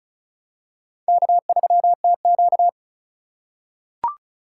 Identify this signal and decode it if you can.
Morse code, heard as a keyed tone, about 36 wpm, 700 Hz. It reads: K3TQ
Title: USA Call Signs - 4 Characters - 3x Character Spacing - 35wpm